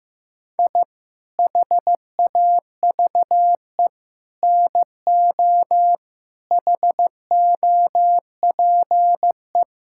Morse code, heard as a keyed tone, about 16 wpm, 700 Hz. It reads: I HAVE NO HOPE
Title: Sentences from Top 1000 Words - Review Part 5 - 15wpm